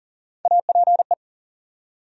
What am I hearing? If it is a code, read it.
Morse code, heard as a keyed tone, about 40 wpm, 700 Hz. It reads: APE
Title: USA Call Sign Postfixes - 3 Letters 40wpm